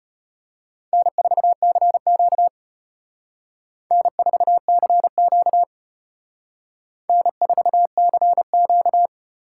Morse code, heard as a keyed tone, about 34 wpm, 700 Hz. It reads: N4CQ N4CQ N4CQ